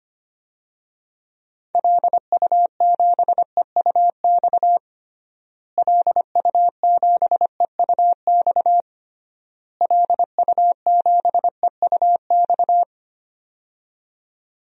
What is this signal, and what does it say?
Morse code, heard as a keyed tone, about 25 wpm, 700 Hz. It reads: LU7EUX LU7EUX LU7EUX
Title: DXCC-100 Call Signs - 6 Characters - Repeated 3x - 25wpm